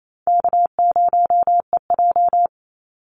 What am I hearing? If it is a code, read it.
Morse code, heard as a keyed tone, about 28 wpm, 700 Hz. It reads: K0EJ
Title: Easy Call Signs Repeated 3x - 28wpm